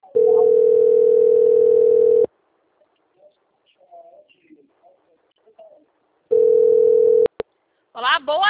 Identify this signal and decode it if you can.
Ringback tone (North America), 440 + 480 Hz, with speech.
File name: out- - - - - .82724